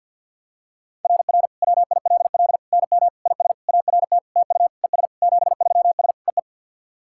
Morse code, heard as a keyed tone, about 47 wpm, 690 Hz. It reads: RR WILL NM IS ART TU ES 73S EE
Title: SOTA QSO Practice 50wpm